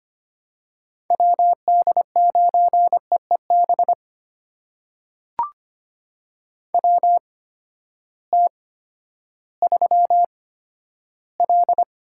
Morse code, heard as a keyed tone, about 25 wpm, spaced as for 12 wpm, 700 Hz. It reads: WD9EEB WT3L
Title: USA Call Signs - 8x Character Spacing - 25wpm